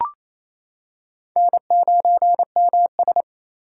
Morse code, heard as a keyed tone, about 28 wpm, 700 Hz. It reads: N9MS
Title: K1USN SST QSO Practice - 28wpm